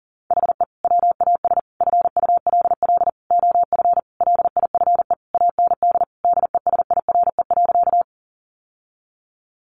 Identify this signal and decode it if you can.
Morse code, heard as a keyed tone, about 40 wpm, 700 Hz. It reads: HE WAS FULL OF LIFE AND DESIRE.
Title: Sentences from Top 1100 Words 40wpm